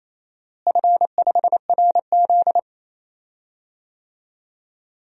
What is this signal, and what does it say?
Morse code, heard as a keyed tone, about 27 wpm, 700 Hz. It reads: F5RZ
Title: DXCC-100 Call Signs - 4 Characters - Repeated 3x - 28wpm